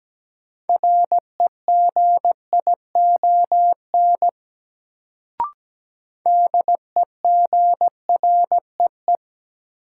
Morse code, heard as a keyed tone, about 17 wpm, 700 Hz. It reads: REGION DEGREE
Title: Sentences from Top 1500 Words - New Word Drill Alt - 17wpm